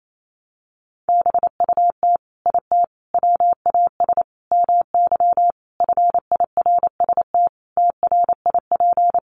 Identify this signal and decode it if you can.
Morse code, heard as a keyed tone, about 28 wpm, 700 Hz. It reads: BUT IT WAS MY FIRST TRIP
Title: Sentences from Top 1300 Words 28wpm